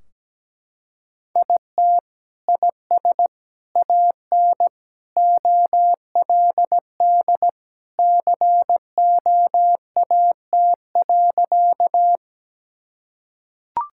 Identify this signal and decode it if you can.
Morse code, heard as a keyed tone, about 17 wpm, 700 Hz. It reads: IT IS AN OLD COAT.
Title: Sentences from Top 1600 Words 17wpm